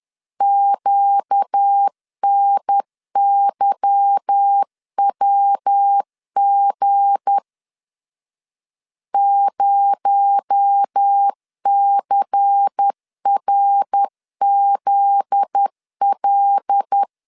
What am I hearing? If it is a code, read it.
Morse code, heard as a keyed tone, about 10 wpm, 800 Hz. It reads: QNYWG 0CRZL
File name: LEZIONI / 8 YJ45 / 8 Y 10wpm 2615